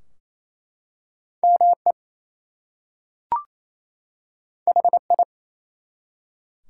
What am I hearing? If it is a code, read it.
Morse code, heard as a keyed tone, about 27 wpm, 700 Hz. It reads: ME HI
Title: Two Letter Words - 28wpm